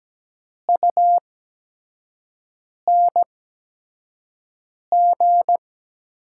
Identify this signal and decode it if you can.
Morse code, heard as a keyed tone, about 17 wpm, spaced as for 5 wpm, 700 Hz. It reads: UNG
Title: 8x Character Spacing - Top 500 Words - 17wpm